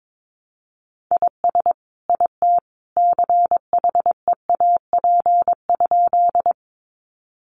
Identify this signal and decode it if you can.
Morse code, heard as a keyed tone, about 22 wpm, 700 Hz. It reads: IS IT CHEAP?
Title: Sentences from Top 1200 Words 22wpm